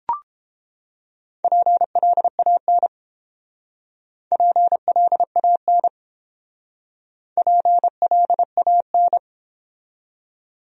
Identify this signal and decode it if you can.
Morse code, heard as a keyed tone, about 30 wpm, 700 Hz. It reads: PLAN PLAN PLAN